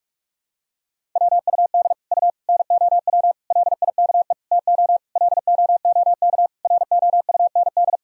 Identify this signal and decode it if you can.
Morse code, heard as a keyed tone, about 46 wpm, 700 Hz. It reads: WUD U NOW LIKE TO LOOK ROUND
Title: Sentences from Top 200 Words and Abbreviations 45wpm